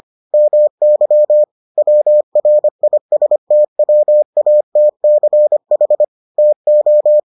Morse code, heard as a keyed tone, about 25 wpm, 600 Hz. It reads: MY WRISTWATCH TO